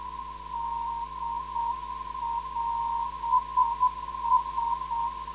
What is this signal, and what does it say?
Morse code, heard as a keyed tone, about 7 wpm, 940 Hz. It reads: DLS
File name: ndb zvok / SE-412kHz